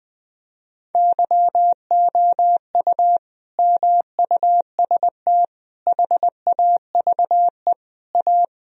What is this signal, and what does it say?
Morse code, heard as a keyed tone, about 20 wpm, 700 Hz. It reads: YOU MUST HAVE A
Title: Sentences from Top 1000 Words - Review Part 4 - 20wpm